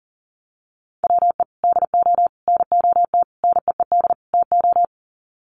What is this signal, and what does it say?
Morse code, heard as a keyed tone, about 40 wpm, 700 Hz. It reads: WE DO NOT NEED TO
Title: Sets of 5 Words Encore 40wpm